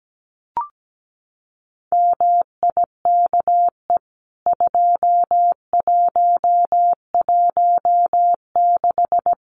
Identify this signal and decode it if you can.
Morse code, heard as a keyed tone, about 17 wpm, 700 Hz. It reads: MIKE 2116
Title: CWT Exchange 17wpm